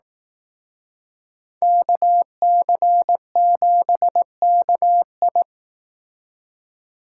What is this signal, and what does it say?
Morse code, heard as a keyed tone, about 18 wpm, 700 Hz. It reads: KC7KI
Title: Speed Racing - USA Call Signs 15wpm